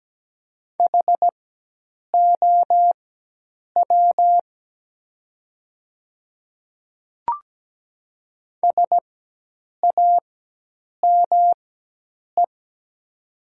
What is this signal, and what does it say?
Morse code, heard as a keyed tone, about 17 wpm, spaced as for 8 wpm, 690 Hz. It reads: HOW SAME
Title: 4x Character Spacing - Top 200 Words - 17wpm